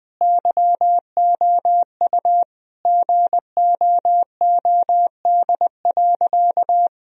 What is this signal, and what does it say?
Morse code, heard as a keyed tone, about 20 wpm, 700 Hz. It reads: YOU GOOD.
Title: Sentences from Top 1000 Words - Review Part 1 - 20wpm